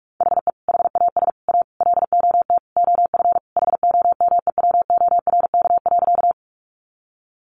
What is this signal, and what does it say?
Morse code, heard as a keyed tone, about 44 wpm, 700 Hz. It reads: HE HAS A LOT OF HOMEWORK.
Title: Additional Easy Phrases 45wpm